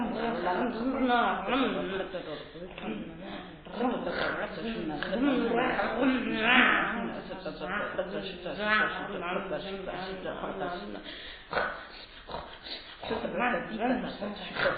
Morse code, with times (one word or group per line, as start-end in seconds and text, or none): none